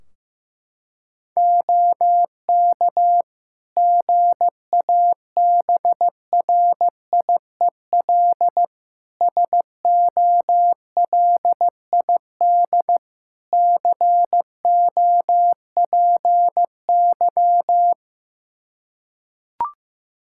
Morse code, heard as a keyed tone, about 15 wpm, 700 Hz. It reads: OK GABRIEL SOLID COPY
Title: Practice QSO - Exchange 3 - More Info - 15wpm